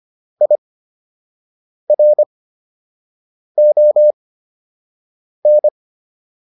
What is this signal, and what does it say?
Morse code, heard as a keyed tone, about 25 wpm, spaced as for 6 wpm, 600 Hz. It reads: IRON
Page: fundamental-practice-files